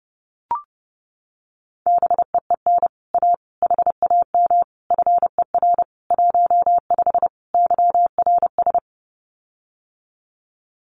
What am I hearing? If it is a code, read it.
Morse code, heard as a keyed tone, about 30 wpm, 700 Hz. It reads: BEEN A HAM FER 15 YRS
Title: Common QSO Phrases Encore 30wpm